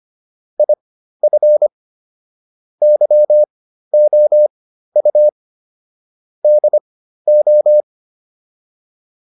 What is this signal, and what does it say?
Morse code, heard as a keyed tone, about 25 wpm, spaced as for 13 wpm, 600 Hz. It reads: IF YOU DO
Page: intermediate-practice-files